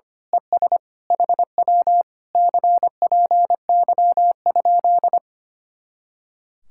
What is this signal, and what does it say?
Morse code, heard as a keyed tone, about 24 wpm, 700 Hz. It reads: ES HW CPY?
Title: Speed-Racing Common QSO Phrases - 25wpm